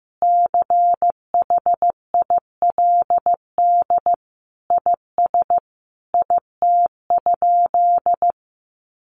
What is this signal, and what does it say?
Morse code, heard as a keyed tone, about 15 wpm, 700 Hz. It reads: CHILD IS IT?
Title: Sentences from Top 1200 Words 15wpm